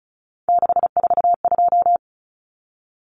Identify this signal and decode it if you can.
Morse code, heard as a keyed tone, about 34 wpm, 700 Hz. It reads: 642
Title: Numbers - 3-Digit 35wpm